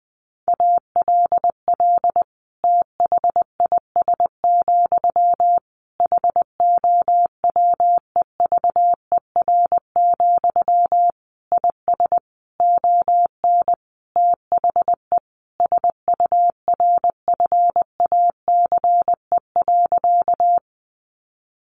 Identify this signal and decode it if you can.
Morse code, heard as a keyed tone, about 20 wpm, 700 Hz. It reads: ALL THIS, HOWEVER, IS ON THE SURFACE.
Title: Sentences from Top 600 Words 20wpm